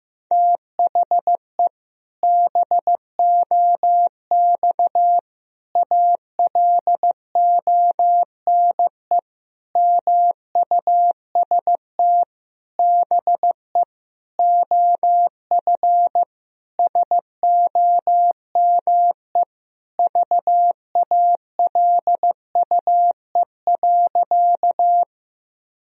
Morse code, heard as a keyed tone, about 16 wpm, 700 Hz. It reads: THE BOX ALONE MUST BE OF SOME VALUE.
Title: Sentences from Top 1000 Words - Review Part 3 - 15wpm